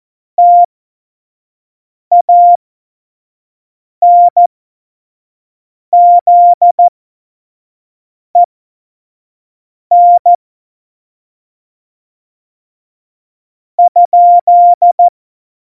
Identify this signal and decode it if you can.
Morse code, heard as a keyed tone, about 14 wpm, spaced as for 4 wpm, 700 Hz. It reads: TANZEN?
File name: Morsekurs / Witze / Witz 04. Dezember 2018, 25 F70